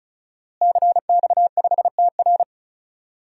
Morse code, heard as a keyed tone, about 34 wpm, 700 Hz. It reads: CX5TR
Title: DXCC-100 Call Signs - 5 Characters - Repeated 3x - 35wpm